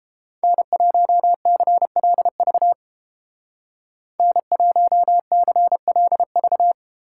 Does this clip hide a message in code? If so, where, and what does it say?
Yes: Morse code from 0.4 to 6.7 s, heard as a keyed tone, about 30 wpm, 700 Hz. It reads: N1CLV N1CLV